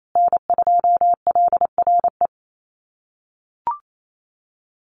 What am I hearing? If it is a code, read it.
Morse code, heard as a keyed tone, about 28 wpm, 700 Hz. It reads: N2LRE
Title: USA Call Signs Repeated 3x 28wpm